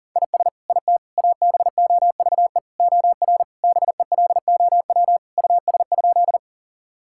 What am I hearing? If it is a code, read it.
Morse code, heard as a keyed tone, about 40 wpm, 700 Hz. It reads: IS IT ABOVE OR BELOW US?
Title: Sentences from Top 1000 Words - Review Part 5 - 40wpm